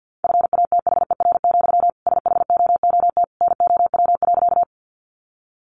Morse code, heard as a keyed tone, about 50 wpm, 700 Hz. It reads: FATHER, SHOOT NOW.
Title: Sentences from Top 1200 Words 50wpm